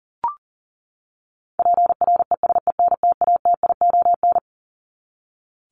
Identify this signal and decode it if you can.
Morse code, heard as a keyed tone, about 40 wpm, 700 Hz. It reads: PRESENTATION
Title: Top 1501-1750 Words 40wpm